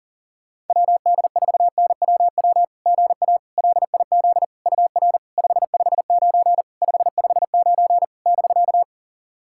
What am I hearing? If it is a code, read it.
Morse code, heard as a keyed tone, about 40 wpm, 690 Hz. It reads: WD4NWW GA LIZ UR 559 559 <BK>
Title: SOTA QSO Practice 40wpm